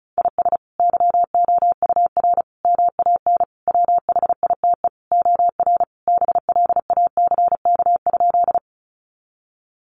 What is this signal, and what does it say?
Morse code, heard as a keyed tone, about 34 wpm, 700 Hz. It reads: IS YOUR MAN WHITE OR BLACK?
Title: Sentences from Top 400 Words 35wpm